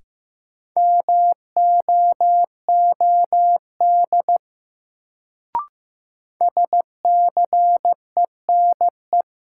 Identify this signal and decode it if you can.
Morse code, heard as a keyed tone, about 15 wpm, 700 Hz. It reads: MOOD SCENE
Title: Sentences from Top 1100 Words - New Word Drill - 15wpm